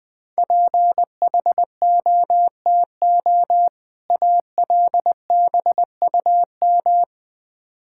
Morse code, heard as a keyed tone, about 20 wpm, 700 Hz. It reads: PHOTO ALBUM
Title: Difficult Words with Hints - 20wpm